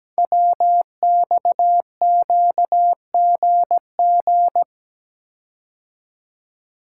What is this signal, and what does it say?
Morse code, heard as a keyed tone, about 17 wpm, 700 Hz. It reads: WXQGG
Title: Code Groups - 5 Letters - 17wpm